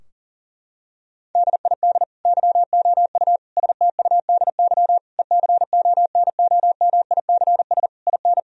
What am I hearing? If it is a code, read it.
Morse code, heard as a keyed tone, about 37 wpm, 700 Hz. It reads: DID YOU STUDY ECONOMICS IN
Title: Sentences from Top 1000 Words - Review Part 3 - 40wpm